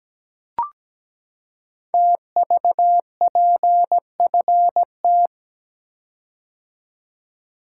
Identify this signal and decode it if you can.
Morse code, heard as a keyed tone, about 17 wpm, 700 Hz. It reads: TVPFT
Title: Code Groups - 5 Letters - 17wpm